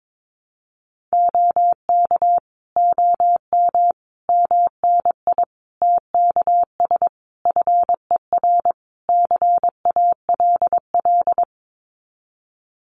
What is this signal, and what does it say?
Morse code, heard as a keyed tone, about 22 wpm, 700 Hz. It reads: OK OM MNI TKS FER CALL
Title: Common QSO Phrases 22wpm